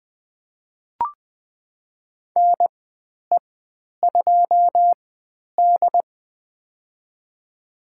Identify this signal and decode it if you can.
Morse code, heard as a keyed tone, about 20 wpm, spaced as for 10 wpm, 700 Hz. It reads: NE2D
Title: 095 Lesson 16g - Letter c - Call Signs 20wpm